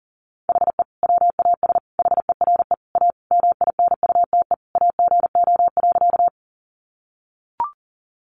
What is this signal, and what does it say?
Morse code, heard as a keyed tone, about 40 wpm, 700 Hz. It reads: HE WAS HERE A MINUTE AGO.
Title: Sentences from Top 500 Words 40wpm